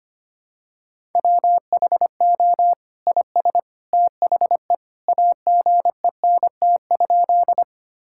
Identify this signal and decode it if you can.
Morse code, heard as a keyed tone, about 25 wpm, 700 Hz. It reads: WHO IS THE AGENT?
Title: Sentences from Top 1600 Words 25wpm